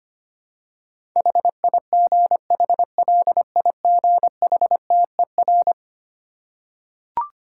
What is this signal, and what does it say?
Morse code, heard as a keyed tone, about 25 wpm, 700 Hz. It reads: HIGHLIGHTER
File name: practice / 25wpm / Difficult Words with Hints - 25wpm